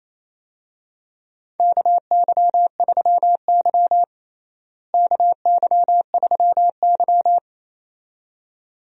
Morse code, heard as a keyed tone, about 28 wpm, 700 Hz. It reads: KY3Y KY3Y
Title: USA Call Signs - 4 Characters - Repeated 2x - 28wpm